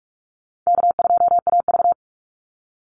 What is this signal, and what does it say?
Morse code, heard as a keyed tone, about 44 wpm, 700 Hz. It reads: K2AV
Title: Easy Call Signs Repeated 3x - 45wpm